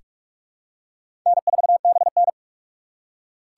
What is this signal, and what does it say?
Morse code, heard as a keyed tone, about 42 wpm, 700 Hz. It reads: N4BN
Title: USA Call Signs - 4 Characters - 4x Character Spacing - 45wpm